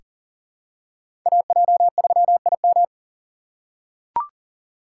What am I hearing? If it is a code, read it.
Morse code, heard as a keyed tone, about 39 wpm, 700 Hz. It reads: AJ3IM
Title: USA Call Signs - 5 Characters - 6x Character Spacing - 40wpm